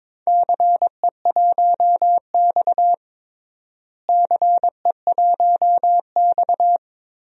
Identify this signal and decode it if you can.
Morse code, heard as a keyed tone, about 22 wpm, 700 Hz. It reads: CE1X CE1X
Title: DXCC-100 Call Signs - 4 Characters - Repeated 2x - 22wpm